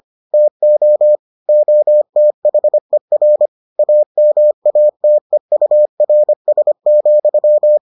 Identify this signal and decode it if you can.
Morse code, heard as a keyed tone, about 25 wpm, 600 Hz. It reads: TO OTHER AMATEURS,